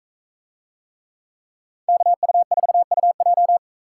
Morse code, heard as a keyed tone, about 44 wpm, 700 Hz. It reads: KU4UJ